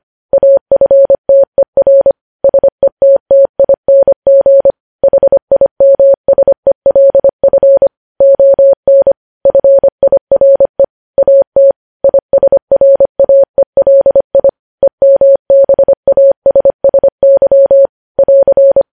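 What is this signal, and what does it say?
Morse code, heard as a keyed tone, about 25 wpm, 550 Hz. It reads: AFTER SETTING HIMSELF ON FIRE AT ISRAELI EMBASSY <AR>
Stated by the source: Morse code news headlines at 25 wpm.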